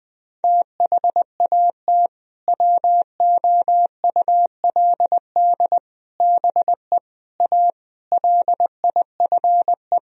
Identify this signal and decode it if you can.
Morse code, heard as a keyed tone, about 20 wpm, 700 Hz. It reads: THAT WOULD BE A LIFE
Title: Sentences from Top 1000 Words - Review Part 4 - 20wpm